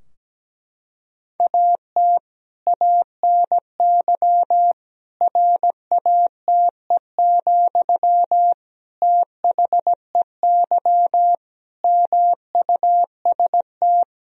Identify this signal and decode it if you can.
Morse code, heard as a keyed tone, about 17 wpm, 700 Hz. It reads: AT ANY RATE, THEY MUST
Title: Sentences from Top 1000 Words - Review Part 4 - 17wpm